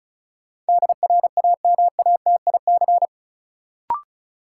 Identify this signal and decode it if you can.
Morse code, heard as a keyed tone, about 34 wpm, 700 Hz. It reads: DRAMATIC
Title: Sentences from Top 1300 Words - New Word Drill - 35wpm